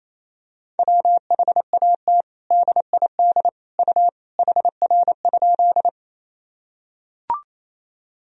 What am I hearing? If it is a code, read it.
Morse code, heard as a keyed tone, about 27 wpm, 700 Hz. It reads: WHAT DID U HR?
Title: Sentences from Top 300 Words and Abbreviations 28wpm